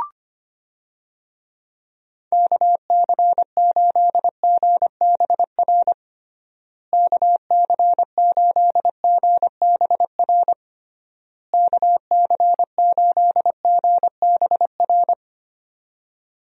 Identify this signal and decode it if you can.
Morse code, heard as a keyed tone, about 25 wpm, 700 Hz. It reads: KC8GBR KC8GBR KC8GBR
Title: USA Call Signs - 6 Characters - Repeated 3x - 25wpm